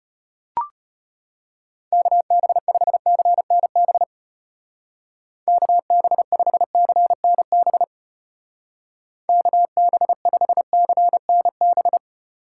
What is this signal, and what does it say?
Morse code, heard as a keyed tone, about 32 wpm, 700 Hz. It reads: KB5CNB KB5CNB KB5CNB